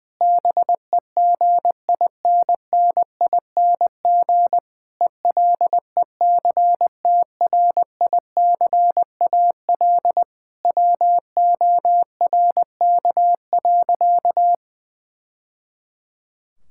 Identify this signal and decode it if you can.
Morse code, heard as a keyed tone, about 20 wpm, 700 Hz. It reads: BEGINNING ELECTRICAL WORK.
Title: Sentences from Top 1000 Words - Review Part 6 - 20wpm